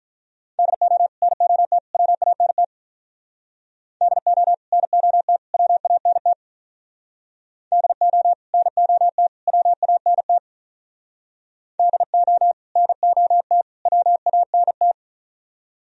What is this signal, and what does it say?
Morse code, heard as a keyed tone, about 43 wpm, 700 Hz. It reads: DO NOT WANT DO NOT WANT DO NOT WANT DO NOT WANT
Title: Speed-Racing - Sets of 3 Words 35wpm